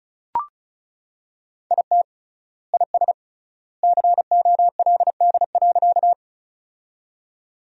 Morse code, heard as a keyed tone, about 36 wpm, 700 Hz. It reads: IT IS COLD.
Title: 3x Word-Spacing - Sentences from Top 400 Words - 35wpm